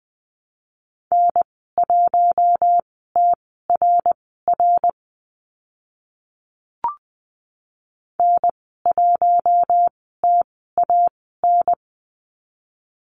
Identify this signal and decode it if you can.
Morse code, heard as a keyed tone, about 20 wpm, spaced as for 14 wpm, 700 Hz. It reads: N1TRR N1TAN
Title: 062 Lesson 11g - Letter h - Call Signs 20wpm